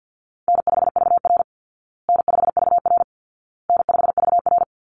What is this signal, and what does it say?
Morse code, heard as a keyed tone, about 50 wpm, 700 Hz. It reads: N5VR N5VR N5VR
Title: Easy Call Signs Repeated 3x - 50wpm